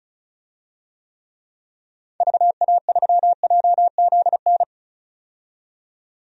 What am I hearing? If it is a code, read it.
Morse code, heard as a keyed tone, about 36 wpm, 700 Hz. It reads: VA3JZN
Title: DXCC-100 Call Signs - 6 Characters - 35wpm